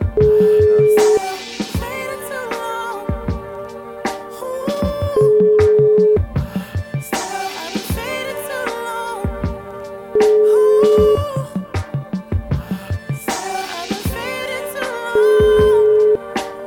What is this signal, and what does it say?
Ringback tone (Europe, 425 Hz).